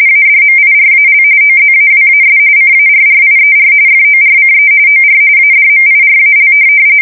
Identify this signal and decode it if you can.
RTTY, about 45.45 baud. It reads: ADMIRAL OZAWAS CARRIER FORCE OR ADMIRAL
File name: fsk170 / USS-New-Jersey